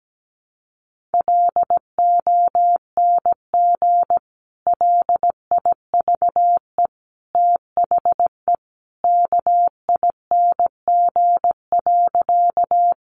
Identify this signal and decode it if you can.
Morse code, heard as a keyed tone, about 17 wpm, 700 Hz. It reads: LONG LIVE THE KING.
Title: Sentences from Top 400 Words 17wpm